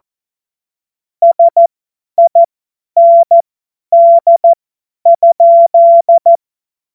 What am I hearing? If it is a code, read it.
Morse code, heard as a keyed tone, about 14 wpm, spaced as for 10 wpm, 680 Hz. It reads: SIND?
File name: Morsekurs / Witze / Witz 11. Juli 2023 50 F70